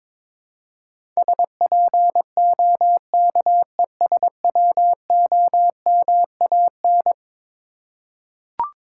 Morse code, heard as a keyed tone, about 22 wpm, 700 Hz. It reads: SPOKESWOMAN
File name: practice / 22wpm / Eleven Letter Words - 22wpm